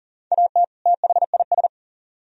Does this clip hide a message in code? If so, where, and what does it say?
Yes: Morse code from 0.3 to 1.7 s, heard as a keyed tone, about 51 wpm, 700 Hz. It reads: AT THIS